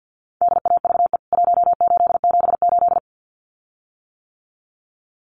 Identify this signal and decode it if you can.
Morse code, heard as a keyed tone, about 50 wpm, 700 Hz. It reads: DAVE 1878
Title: CWT Exchange 50wpm